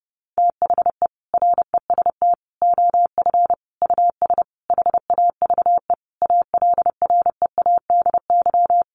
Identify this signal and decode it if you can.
Morse code, heard as a keyed tone, about 30 wpm, 700 Hz. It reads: THE REST OF US HAVE ALREADY
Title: Sentences from Top 700 Words 30wpm